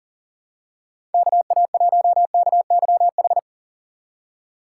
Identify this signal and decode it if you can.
Morse code, heard as a keyed tone, about 37 wpm, 700 Hz. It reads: KA1KYH
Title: USA Call Signs - 6 Characters - 6x Character Spacing - 40wpm